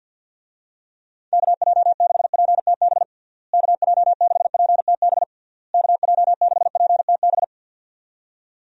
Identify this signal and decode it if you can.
Morse code, heard as a keyed tone, about 51 wpm, 700 Hz. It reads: KJ6PTB KJ6PTB KJ6PTB
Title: USA Call Signs - 6 Characters - Repeated 3x - 50wpm